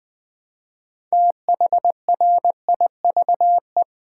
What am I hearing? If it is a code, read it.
Morse code, heard as a keyed tone, about 20 wpm, 700 Hz. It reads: THRIVE
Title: Build A Word - 2 Letters At A Time - 6-Letter Words - 20wpm